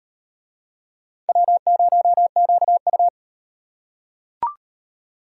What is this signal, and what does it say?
Morse code, heard as a keyed tone, about 38 wpm, 700 Hz. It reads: W0QU